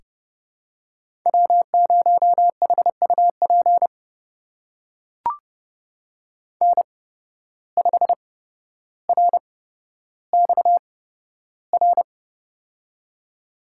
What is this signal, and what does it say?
Morse code, heard as a keyed tone, about 30 wpm, spaced as for 13 wpm, 700 Hz. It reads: W0HUP N5RXR
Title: USA Call Signs - 5 Characters - 8x Character Spacing - 30wpm